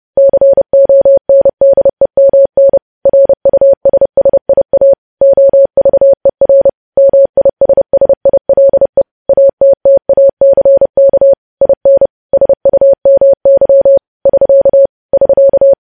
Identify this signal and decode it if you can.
Morse code, heard as a keyed tone, about 30 wpm, 550 Hz. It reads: CONDEMN RUSSIA OVER MISSILE ATTACK IN SUMY <SK> <SK>
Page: news headlines in morse code at 30 wpm